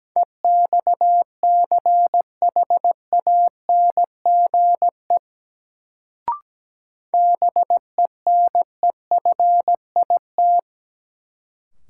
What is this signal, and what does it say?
Morse code, heard as a keyed tone, about 17 wpm, 700 Hz. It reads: EXCHANGE BENEFIT
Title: Sentences from Top 1000 Words - Review Part 6 - New Word Drill - 17wpm